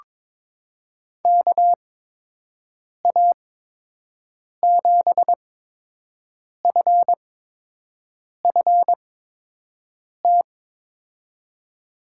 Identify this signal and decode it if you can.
Morse code, heard as a keyed tone, about 22 wpm, spaced as for 6 wpm, 700 Hz. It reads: KA7FFT
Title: USA Call Signs - 6 Characters - 8x Character Spacing - 22wpm